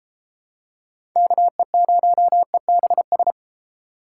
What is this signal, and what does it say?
Morse code, heard as a keyed tone, about 32 wpm, 700 Hz. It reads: KE0EBS